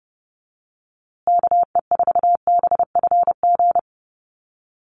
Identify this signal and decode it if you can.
Morse code, heard as a keyed tone, about 30 wpm, 700 Hz. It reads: KE4BFG
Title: USA Call Signs Repeated 3x 30wpm